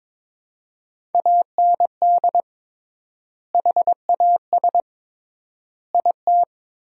Morse code, heard as a keyed tone, about 22 wpm, 700 Hz. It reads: AND HAS IT
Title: 3x Word-Spacing - Sentences from Top 100 Words - 22wpm